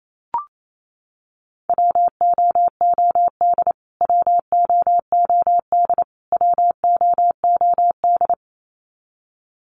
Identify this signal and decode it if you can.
Morse code, heard as a keyed tone, about 28 wpm, 700 Hz. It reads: WOOD WOOD WOOD
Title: Top 300 Words Repeated 28wpm